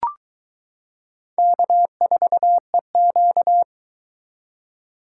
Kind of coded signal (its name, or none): Morse code